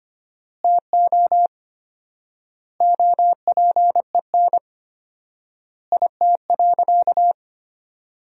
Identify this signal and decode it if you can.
Morse code, heard as a keyed tone, about 25 wpm, 700 Hz. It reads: TO OPEN IT.
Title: 4x Word-Spacing - Sentences from Top 300 Words - 25wpm